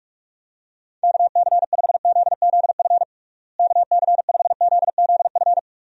Morse code, heard as a keyed tone, about 46 wpm, 700 Hz. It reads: KC5ZZF KC5ZZF
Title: USA Call Signs - 6 Characters - Repeated 3x - 45wpm